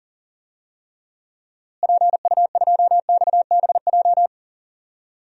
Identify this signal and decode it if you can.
Morse code, heard as a keyed tone, about 40 wpm, 700 Hz. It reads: PU2XBJ
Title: DXCC-100 Call Signs - 6 Characters - Repeated 2x - 40wpm